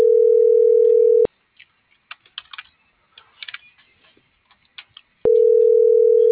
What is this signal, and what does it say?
Ringback tone (North America), 440 + 480 Hz.